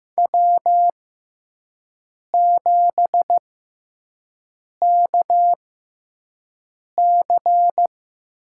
Morse code, heard as a keyed tone, about 15 wpm, spaced as for 5 wpm, 700 Hz. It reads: W7KC
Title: USA Call Signs - 5 Characters - 6x Character Spacing - 15wpm